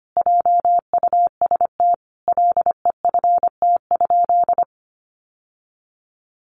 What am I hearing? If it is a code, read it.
Morse code, heard as a keyed tone, about 25 wpm, 700 Hz. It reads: JUST LEFT?
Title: Sentences from Top 1200 Words 25wpm